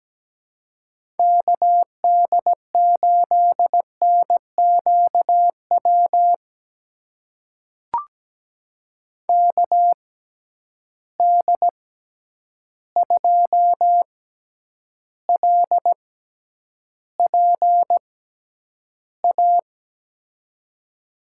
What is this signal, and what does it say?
Morse code, heard as a keyed tone, about 17 wpm, spaced as for 9 wpm, 700 Hz. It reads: KD8NQW KD2LPA
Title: USA Call Signs - 6 Characters - 6x Character Spacing - 17wpm